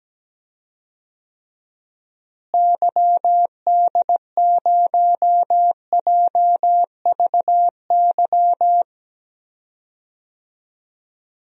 Morse code, heard as a keyed tone, about 17 wpm, 700 Hz. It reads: YD0JVY
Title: DXCC-100 Call Signs - 6 Characters - Repeated 2x - 17wpm